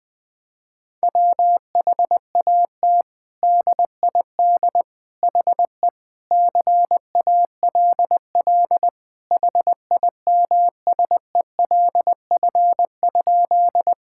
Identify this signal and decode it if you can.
Morse code, heard as a keyed tone, about 20 wpm, 700 Hz. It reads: WHAT DID HE CALL HIMSELF?
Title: Sentences from Top 1600 Words 20wpm